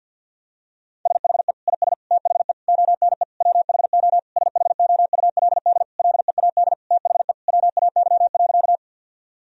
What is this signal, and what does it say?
Morse code, heard as a keyed tone, about 50 wpm, 700 Hz. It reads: SHE IS THE ONE WHO SHOULD LEAD THE WAY.
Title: Sentences from Top 1000 Words - Review Part 2 - 50wpm